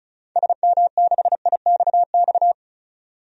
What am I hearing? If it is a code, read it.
Morse code, heard as a keyed tone, about 34 wpm, 700 Hz. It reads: SM6IXX
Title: DXCC-100 Call Signs - 6 Characters - 35wpm